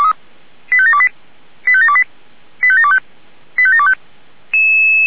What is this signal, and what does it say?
Five-tone selective call (ZVEI), heard as quick runs of five tones, tones of about 80 ms. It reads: 85728 85728 85725 85725